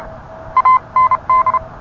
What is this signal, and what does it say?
Morse code, heard as a keyed tone, about 31 wpm, spaced as for 27 wpm, 1030 Hz. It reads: AND